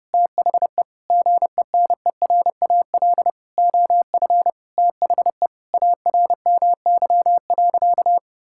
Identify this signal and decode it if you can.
Morse code, heard as a keyed tone, about 32 wpm, 700 Hz. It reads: THE GENERAL OF THE ARMY.
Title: Sentences from Top 1600 Words 30wpm